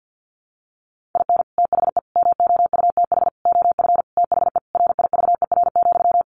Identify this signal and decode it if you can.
Morse code, heard as a keyed tone, about 50 wpm, 700 Hz. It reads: IN THE MOUTH OF THE RIVER,
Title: Sentences from Top 1500 Words 50wpm